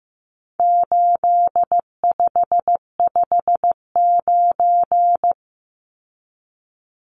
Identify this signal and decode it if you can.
Morse code, heard as a keyed tone, about 15 wpm, 700 Hz. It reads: 8559
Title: Numbers - 4-Digit 15wpm